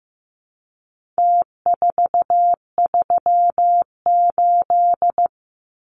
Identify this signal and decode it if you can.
Morse code, heard as a keyed tone, about 15 wpm, 700 Hz. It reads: T438
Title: Numbers - Contest Style - 4-Digit 15wpm